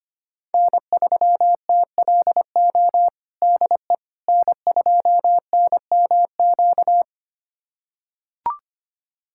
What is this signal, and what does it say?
Morse code, heard as a keyed tone, about 25 wpm, 700 Hz. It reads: N3TLO DE N2NMQ
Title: USA Callsign DE Callsign - 5 Characters - 25wpm